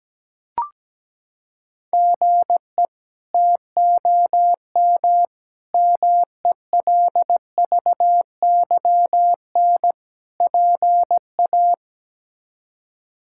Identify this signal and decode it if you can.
Morse code, heard as a keyed tone, about 17 wpm, 700 Hz. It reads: GE TOM MELVYN PA
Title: K1USN SST QSO Practice - 17wpm